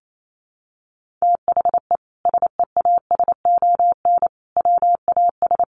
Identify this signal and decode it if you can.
Morse code, heard as a keyed tone, about 28 wpm, 700 Hz. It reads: THE SEASON WAS
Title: Sentences from Top 1400 Words 28wpm